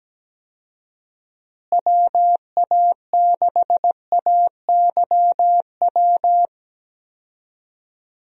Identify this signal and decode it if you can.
Morse code, heard as a keyed tone, about 17 wpm, 700 Hz. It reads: WA6AYW
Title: USA Call Signs - 6 Characters - Repeated 2x - 17wpm